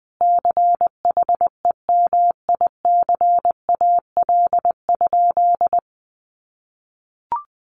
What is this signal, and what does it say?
Morse code, heard as a keyed tone, about 20 wpm, 700 Hz. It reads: CHEMICAL?
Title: Sentences from Top 1200 Words 20wpm